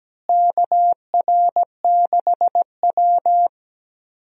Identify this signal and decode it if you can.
Morse code, heard as a keyed tone, about 17 wpm, 700 Hz. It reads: KR6W
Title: USA Call Signs - 4 Characters - Repeated 2x - 17wpm